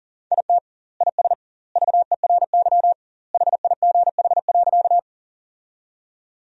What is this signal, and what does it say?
Morse code, heard as a keyed tone, about 40 wpm, 700 Hz. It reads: IT IS VERY HIGH.
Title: 2x Word-Spacing - Sentences from Top 200 Words - 40wpm